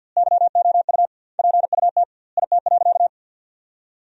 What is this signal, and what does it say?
Morse code, heard as a keyed tone, about 51 wpm, 700 Hz. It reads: YOU PUT IT.
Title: 2x Word-Spacing - Sentences from Top 200 Words - 50wpm